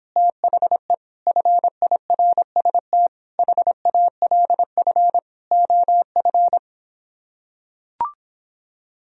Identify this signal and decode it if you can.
Morse code, heard as a keyed tone, about 26 wpm, 700 Hz. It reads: THE FIRST HALF OF